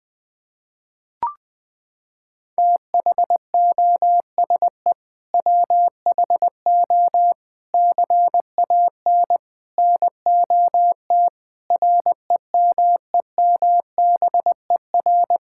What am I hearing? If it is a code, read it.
Morse code, heard as a keyed tone, about 20 wpm, 700 Hz. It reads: THOSE WHO CAN NOT REMEMBER
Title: Sets of 8 Words - 20wpm